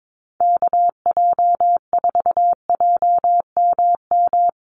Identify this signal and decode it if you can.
Morse code, heard as a keyed tone, about 22 wpm, 700 Hz. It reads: KJ4JMM
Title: USA Call Signs Repeated 3x 22wpm